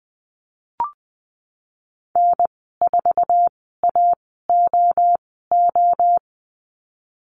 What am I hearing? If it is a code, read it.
Morse code, heard as a keyed tone, about 20 wpm, spaced as for 15 wpm, 700 Hz. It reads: N4AOO
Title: 062 Lesson 11g - Letter h - Call Signs 20wpm